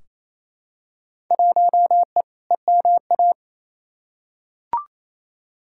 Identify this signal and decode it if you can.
Morse code, heard as a keyed tone, about 28 wpm, 700 Hz. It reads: 1E EMA
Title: Field Day Exchanges Encore - 28wpm